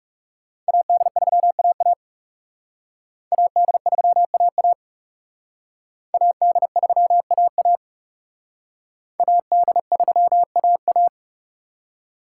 Morse code, heard as a keyed tone, about 35 wpm, 700 Hz. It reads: AD3AA AD3AA AD3AA AD3AA
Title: Speed Racing - USA 2x2 Call Signs 30wpm